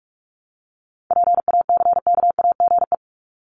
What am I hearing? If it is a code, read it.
Morse code, heard as a keyed tone, about 44 wpm, 700 Hz. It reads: PACKAGE
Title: Top 1501-1750 Words 45wpm